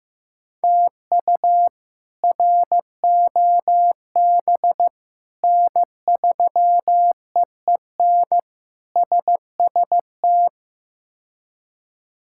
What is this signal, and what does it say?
Morse code, heard as a keyed tone, about 15 wpm, 700 Hz. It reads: TU ROB N3EEN SST
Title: K1USN SST QSO Practice - 15wpm